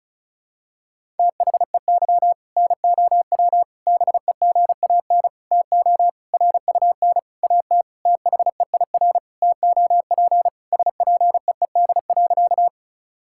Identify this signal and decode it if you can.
Morse code, heard as a keyed tone, about 36 wpm, 700 Hz. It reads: THEY NOW BEGAN TO RUN AT THEIR TOP SPEED.
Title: Sentences from Top 1000 Words - Review Part 5 - 35wpm